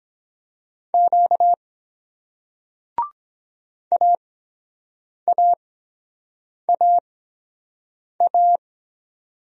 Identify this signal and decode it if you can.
Morse code, heard as a keyed tone, about 23 wpm, spaced as for 7 wpm, 700 Hz. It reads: Q AAAA